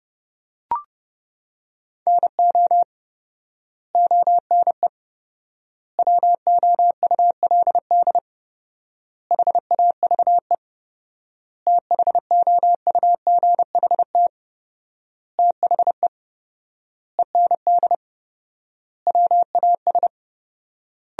Morse code, heard as a keyed tone, about 30 wpm, 700 Hz. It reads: NO ONE WOULD HAVE THOUGHT THE END WAS
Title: 4x Word-Spacing - Sentences from Top 300 Words - 30wpm